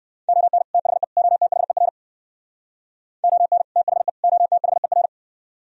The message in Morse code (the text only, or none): ON THE OTHER ON THE OTHER